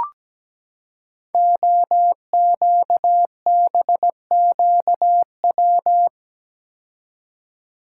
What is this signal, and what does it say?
Morse code, heard as a keyed tone, about 17 wpm, 700 Hz. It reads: OQBQW
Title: Code Groups - 5 Letters - 17wpm